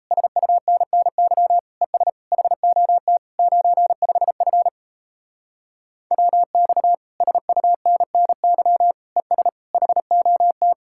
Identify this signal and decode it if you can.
Morse code, heard as a keyed tone, about 37 wpm, 700 Hz. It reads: SUNNY ES HOT 95F WX SUNNY ES HOT